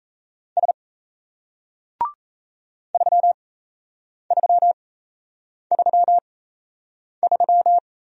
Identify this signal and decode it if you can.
Morse code, heard as a keyed tone, about 38 wpm, spaced as for 8 wpm, 700 Hz. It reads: S 3333